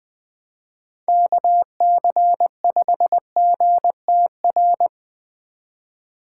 Morse code, heard as a keyed tone, about 20 wpm, 700 Hz. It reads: KC5GTR
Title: Speed Racing - USA 2x3 Call Signs 17wpm